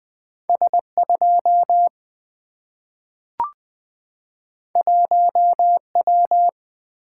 Morse code, heard as a keyed tone, about 20 wpm, 700 Hz. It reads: S2 1W
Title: Single-Letter Single-Number 20wpm